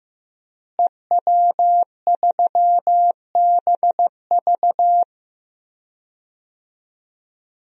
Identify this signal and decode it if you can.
Morse code, heard as a keyed tone, about 15 wpm, 700 Hz. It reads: EW3BV
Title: DXCC-100 Call Signs - 5 Characters - Repeated 2x - 15wpm